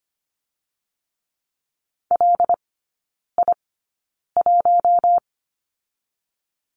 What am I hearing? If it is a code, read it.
Morse code, heard as a keyed tone, about 25 wpm, spaced as for 9 wpm, 700 Hz. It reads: LI1
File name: Lesson 15e - Number 5 - Three Characters 25wpm